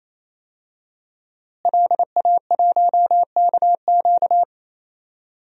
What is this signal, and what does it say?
Morse code, heard as a keyed tone, about 28 wpm, 700 Hz. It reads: LA1KQ
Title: DXCC-100 Call Signs - 5 Characters - Repeated 3x - 28wpm